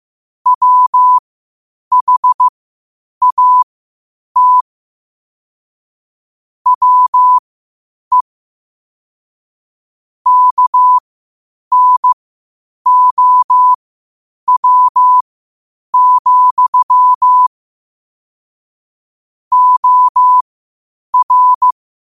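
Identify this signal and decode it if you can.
Morse code, heard as a keyed tone, about 15 wpm, spaced as for 8 wpm, 1000 Hz. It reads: WHAT WE KNOW, OR
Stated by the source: Morse practice at 8 WPM from Morse Resource.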